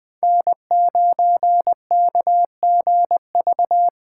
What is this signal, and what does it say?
Morse code, heard as a keyed tone, about 20 wpm, 700 Hz. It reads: N9KGV
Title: Speed Racing - USA 1x3 Call Signs 17wpm